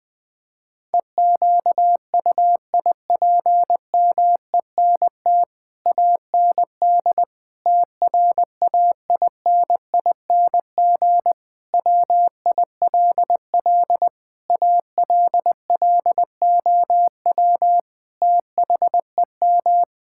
Morse code, heard as a keyed tone, about 20 wpm, 700 Hz. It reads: EQUIPMENT AND TRAINING WILL ALLOW THEM
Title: Sentences from Top 1000 Words - Review Part 5 - 20wpm